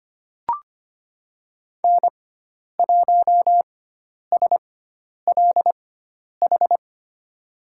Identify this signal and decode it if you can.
Morse code, heard as a keyed tone, about 25 wpm, spaced as for 10 wpm, 700 Hz. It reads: N1SLH
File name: Lesson 16g - Letter c - Call Signs 25wpm